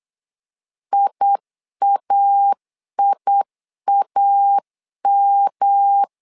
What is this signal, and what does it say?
Morse code, heard as a keyed tone, about 8 wpm, 800 Hz. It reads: IAIAM
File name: LEZIONI / 3 MWRO / 3 MWR 8wpm 2444